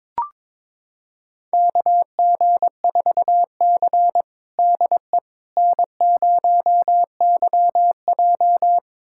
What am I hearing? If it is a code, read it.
Morse code, heard as a keyed tone, about 22 wpm, 700 Hz. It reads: KG4C DE N0YJ
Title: USA Callsign DE Callsign - 4 Characters - 22wpm